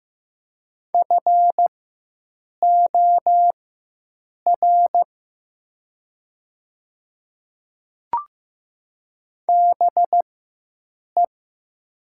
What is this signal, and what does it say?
Morse code, heard as a keyed tone, about 15 wpm, spaced as for 7 wpm, 700 Hz. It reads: FOR BE
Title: 4x Character Spacing - Top 200 Words - 15wpm